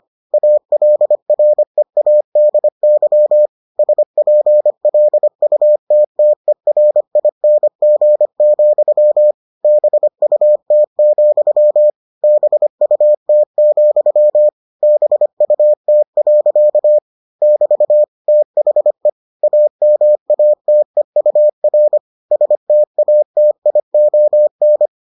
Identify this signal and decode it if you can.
Morse code, heard as a keyed tone, about 25 wpm, 600 Hz. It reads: ALREADY SPLUTTERING, BUT, BUT, BUT. <BT> THE AMATEUR STATION